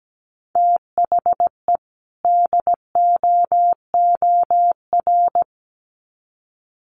Morse code, heard as a keyed tone, about 17 wpm, 700 Hz. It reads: THE DOOR
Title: Sets of 2 Words 17wpm